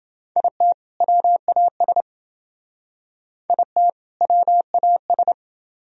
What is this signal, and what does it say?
Morse code, heard as a keyed tone, about 28 wpm, 700 Hz. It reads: IT WAS IT WAS